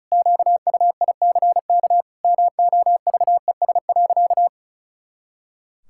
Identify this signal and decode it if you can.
Morse code, heard as a keyed tone, about 34 wpm, 700 Hz. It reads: QUICK MOVES.
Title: Pangram Challenge v2 - 35wpm